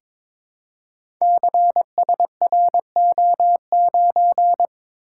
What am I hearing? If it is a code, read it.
Morse code, heard as a keyed tone, about 22 wpm, 700 Hz. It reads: CSRO9
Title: Code Groups - 5 Letter:Numbers - 22wpm